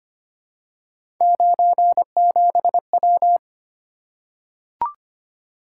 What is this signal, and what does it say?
Morse code, heard as a keyed tone, about 25 wpm, 700 Hz. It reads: 97W
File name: Three Letter-Number Combinations - 25wpm